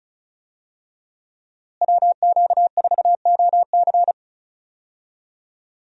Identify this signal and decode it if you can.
Morse code, heard as a keyed tone, about 36 wpm, 700 Hz. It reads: WQ4OC
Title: USA Call Signs - 5 Characters - Repeated 3x - 35wpm